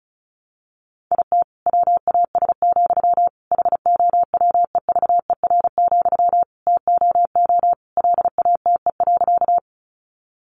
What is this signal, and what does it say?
Morse code, heard as a keyed tone, about 34 wpm, 700 Hz. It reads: IT WAS, HOWEVER, TOO LATE.
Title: Sentences from Top 600 Words 35wpm